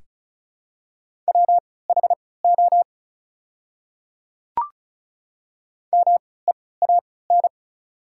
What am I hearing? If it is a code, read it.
Morse code, heard as a keyed tone, about 36 wpm, spaced as for 20 wpm, 700 Hz. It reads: WHO MEAN
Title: 3x Character Spacing - Top 200 Words - 35wpm